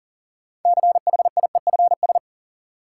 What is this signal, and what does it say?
Morse code, heard as a keyed tone, about 38 wpm, 700 Hz. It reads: CHIEFS
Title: Sets of 5 Words Encore - New Word Drill - 40wpm